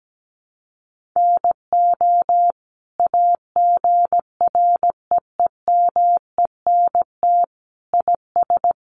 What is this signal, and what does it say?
Morse code, heard as a keyed tone, about 17 wpm, 700 Hz. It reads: NO AGREEMENT IS
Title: Sentences from Top 1500 Words 17wpm